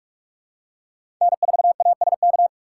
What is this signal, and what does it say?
Morse code, heard as a keyed tone, about 46 wpm, 700 Hz. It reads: N4ASK
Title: USA Call Signs - 5 Characters - 6x Character Spacing - 45wpm